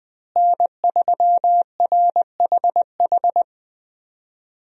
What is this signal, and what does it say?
Morse code, heard as a keyed tone, about 20 wpm, 700 Hz. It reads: N3RHH
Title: USA Call Signs - 5 Characters - Repeated 2x - 20wpm